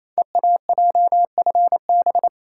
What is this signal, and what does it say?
Morse code, heard as a keyed tone, about 28 wpm, 700 Hz. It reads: EAJFB
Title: Code Groups - 5 Letter:Numbers - 28wpm